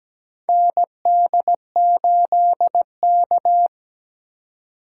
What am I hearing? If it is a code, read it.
Morse code, heard as a keyed tone, about 17 wpm, 700 Hz. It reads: ND8K
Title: USA Call Signs - 4x Character Spacing - 17wpm